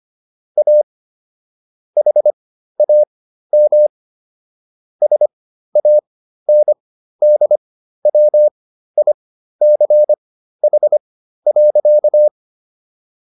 Morse code, heard as a keyed tone, about 25 wpm, spaced as for 13 wpm, 600 Hz. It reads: A HAM SANDWICH.
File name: SS109 13